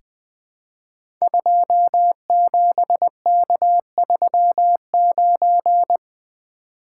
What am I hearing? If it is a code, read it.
Morse code, heard as a keyed tone, about 21 wpm, 700 Hz. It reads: 27K39
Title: Code Groups - 5 Letter:Numbers - 20wpm